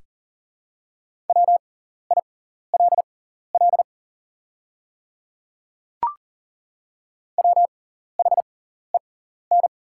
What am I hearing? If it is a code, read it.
Morse code, heard as a keyed tone, about 40 wpm, spaced as for 14 wpm, 700 Hz. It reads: WILL WHEN
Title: 6x Character Spacing - Top 100 Words - 40wpm